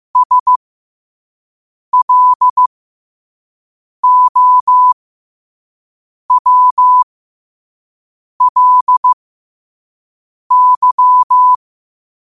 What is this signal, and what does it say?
Morse code, heard as a keyed tone, about 15 wpm, spaced as for 5 wpm, 1000 Hz. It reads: SLOWLY
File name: qotdfiles / 5wpm